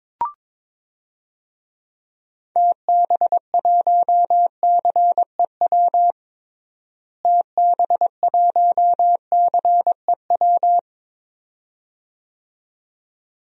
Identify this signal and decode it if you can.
Morse code, heard as a keyed tone, about 22 wpm, 700 Hz. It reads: TB1CEW TB1CEW
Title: DXCC-100 Call Signs - 6 Characters - Repeated 2x - 22wpm